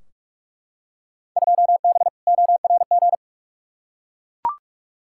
Morse code, heard as a keyed tone, about 42 wpm, 700 Hz. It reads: 2B ORG
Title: Field Day Exchanges Encore - 45wpm